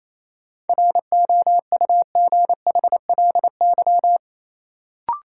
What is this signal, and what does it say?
Morse code, heard as a keyed tone, about 27 wpm, 700 Hz. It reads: ROUGHLY
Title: Sentences from Top 1500 Words - New Word Drill Alt - 28wpm